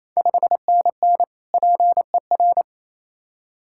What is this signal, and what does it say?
Morse code, heard as a keyed tone, about 27 wpm, 700 Hz. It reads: 5NN PER
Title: Ontario QSO Party Exchange 28wpm